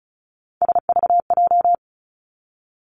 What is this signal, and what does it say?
Morse code, heard as a keyed tone, about 36 wpm, 700 Hz. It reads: SVJ
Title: Three Challenging Letter-Number Combinations 35wpm